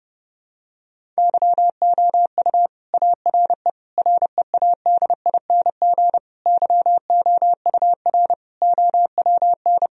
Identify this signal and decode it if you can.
Morse code, heard as a keyed tone, about 30 wpm, 700 Hz. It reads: YOU ARE READING YOUR OWN
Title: Sentences from Top 1000 Words - Review Part 3 - 30wpm